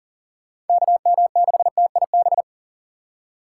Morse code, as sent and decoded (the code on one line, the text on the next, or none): -.- -- -.... - .. -...
KM6TIB